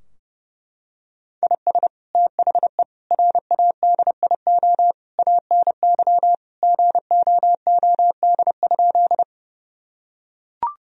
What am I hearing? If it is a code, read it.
Morse code, heard as a keyed tone, about 30 wpm, 700 Hz. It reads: IS THE RADIO ANY GOOD?
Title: Sentences from Top 1000 Words - Review Part 3 - 30wpm